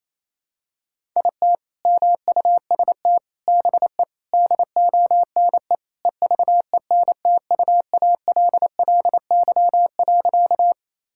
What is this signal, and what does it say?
Morse code, heard as a keyed tone, about 28 wpm, 700 Hz. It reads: IT MUST BE DONE EVENTUALLY.
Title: Sentences from Top 1000 Words - Review Part 6 - 28wpm